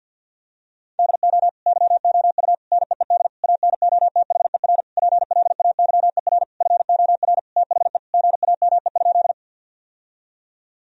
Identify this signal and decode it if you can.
Morse code, heard as a keyed tone, about 51 wpm, 700 Hz. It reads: DO YOU NEED ANOTHER PLAYER FOR THE GAME?
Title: Sentences from Top 1000 Words - Review Part 4 - 50wpm